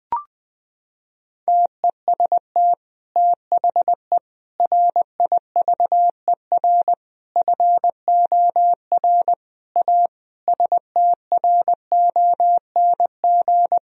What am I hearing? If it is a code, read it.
Morse code, heard as a keyed tone, about 21 wpm, 700 Hz. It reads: TEST THE RIVER FOR A STRONG
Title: Sentences from Top 1000 Words - Review Part 4 - 20wpm